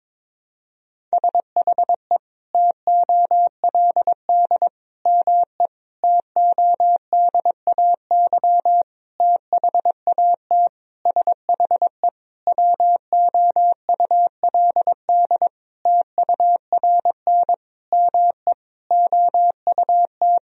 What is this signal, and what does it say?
Morse code, heard as a keyed tone, about 22 wpm, 700 Hz. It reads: SHE TOLD ME TODAY THAT SHE WOULD TURN ME OUT
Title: Sentences from Top 1000 Words - Review Part 4 - 22wpm